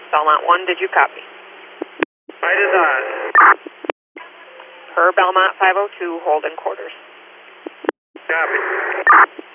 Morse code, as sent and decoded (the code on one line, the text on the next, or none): none